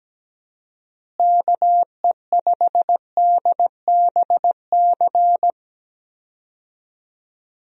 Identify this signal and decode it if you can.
Morse code, heard as a keyed tone, about 17 wpm, 700 Hz. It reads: KE5DBC
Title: USA Call Signs - 6 Characters - Repeated 2x - 17wpm